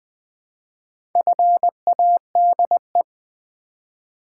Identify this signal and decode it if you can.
Morse code, heard as a keyed tone, about 20 wpm, 700 Hz. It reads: FADE
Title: Build A Word - 2 Letters At A Time - 4-Letter Words - 20wpm